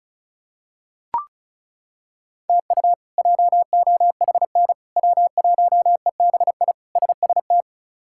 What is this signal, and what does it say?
Morse code, heard as a keyed tone, about 34 wpm, 700 Hz. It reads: TU JOHN W1EBI SST
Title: K1USN SST QSO Practice - 35wpm